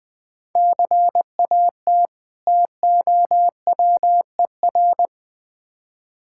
Morse code, heard as a keyed tone, about 21 wpm, 700 Hz. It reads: CAT TOWER
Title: Difficult Words with Hints - 20wpm